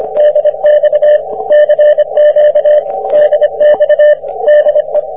Morse code, heard as a keyed tone, about 25 wpm, 600 Hz. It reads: DX CQ DX DE
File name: wp / bd4kuu 40m cw